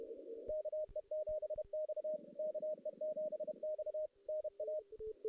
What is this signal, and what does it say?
Morse code, heard as a keyed tone, about 32 wpm, 620 Hz. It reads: KE7X KE7X NA
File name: 2016 JAN CW NAQP / 2016 Jan naqp k5ot - 0-5